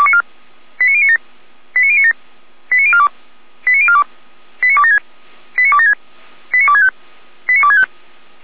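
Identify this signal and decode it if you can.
Five-tone selective call (ZVEI), heard as quick runs of five tones, tones of about 70 ms. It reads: 89097 89097 89042 89042 89276 89276 89265 89265
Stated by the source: ZVEI five-tone selective call 89277 received by SDR, from the Feuerwehr Orlamünde audio archive.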